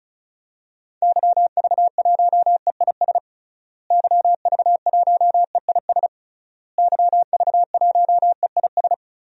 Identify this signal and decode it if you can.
Morse code, heard as a keyed tone, about 34 wpm, 700 Hz. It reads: YV1EIS YV1EIS YV1EIS
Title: DXCC-100 Call Signs - 6 Characters - Repeated 3x - 35wpm